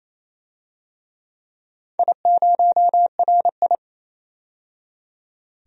Morse code, heard as a keyed tone, about 27 wpm, 690 Hz. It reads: I0RI
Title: DXCC-100 Call Signs - 4 Characters - 28wpm